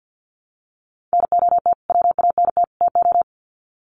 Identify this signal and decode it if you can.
Morse code, heard as a keyed tone, about 50 wpm, 700 Hz. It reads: NOT WANT TO
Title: Sets of 3 Words 50wpm